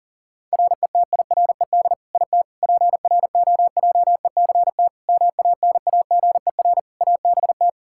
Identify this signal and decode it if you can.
Morse code, heard as a keyed tone, about 40 wpm, 700 Hz. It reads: RETIRED IT PROJECT MANAGER ABT
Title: Practice QSO - Exchange 3 - More Info - 40wpm